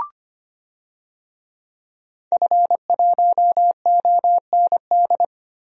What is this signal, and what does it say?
Morse code, heard as a keyed tone, about 25 wpm, 700 Hz. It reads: F1OND
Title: DXCC-100 Call Signs - 5 Characters - Repeated 2x - 25wpm